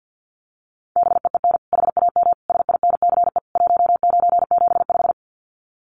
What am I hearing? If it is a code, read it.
Morse code, heard as a keyed tone, about 50 wpm, 700 Hz. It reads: BEEN HAM SINCE 1975
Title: Common QSO Phrases 50wpm